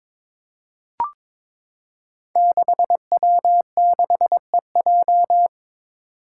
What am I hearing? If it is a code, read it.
Morse code, heard as a keyed tone, about 22 wpm, 700 Hz. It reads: 6W6EJ
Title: Five Challenging Letter-Number Combinations 22wpm